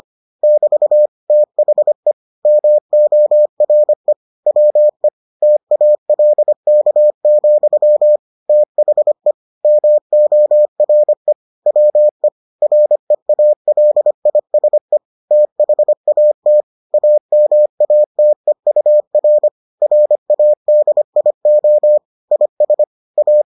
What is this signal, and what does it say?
Morse code, heard as a keyed tone, about 25 wpm, 600 Hz. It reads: <BT> THE MORE WE TALK, THE MORE WE REALISE THAT AMATEUR RADIO IS A